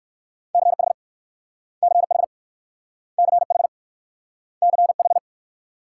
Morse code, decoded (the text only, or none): CH CH CH CH